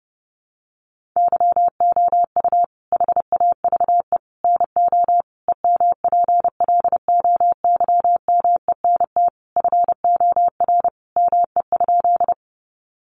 Morse code, heard as a keyed tone, about 30 wpm, 700 Hz. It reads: YOU HAVE NO EMPLOYMENT FOR ME?
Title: Sentences from Top 1500 Words 30wpm